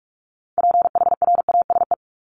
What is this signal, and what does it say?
Morse code, heard as a keyed tone, about 44 wpm, 700 Hz. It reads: PHRASE
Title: Top 1501-1750 Words 45wpm